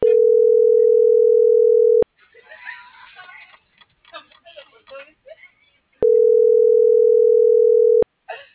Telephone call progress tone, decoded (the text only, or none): ringback tone